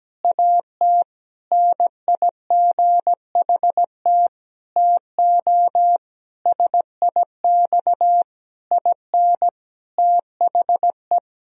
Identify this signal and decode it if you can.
Morse code, heard as a keyed tone, about 17 wpm, 700 Hz. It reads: AT NIGHT TO SIX IN THE
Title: Sentences from Top 1000 Words - Review Part 2 - 17wpm